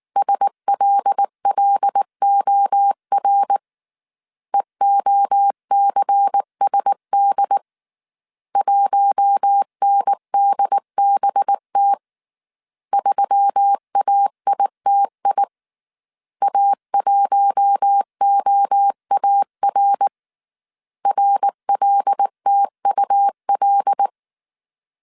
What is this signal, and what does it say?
Morse code, heard as a keyed tone, about 19 wpm, 800 Hz. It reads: SLLOR EOCSD 1NDBT 3AITI A1OAR RLTUL
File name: CW / DET / 1 6 1500C S A 18wpm 2255